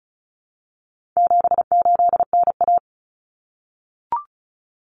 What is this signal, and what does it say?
Morse code, heard as a keyed tone, about 34 wpm, 700 Hz. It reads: 78NA
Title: Common Cut Numbers - 4-Digit 35wpm